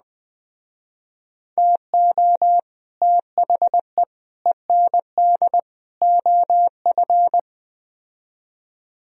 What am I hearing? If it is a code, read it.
Morse code, heard as a keyed tone, about 20 wpm, 700 Hz. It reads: TO THE END OF
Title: Speed-Racing - Sets of 4 Words 17wpm